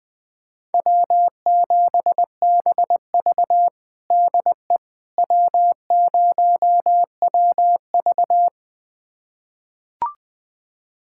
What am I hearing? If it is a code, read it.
Morse code, heard as a keyed tone, about 20 wpm, 700 Hz. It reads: W7BV DE W0WV
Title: USA Callsign DE Callsign - 4 Characters - 20wpm